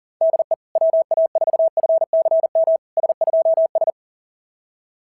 Morse code, heard as a keyed tone, about 42 wpm, 690 Hz. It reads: DE WA4FCM S2S
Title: SOTA QSO Practice 40wpm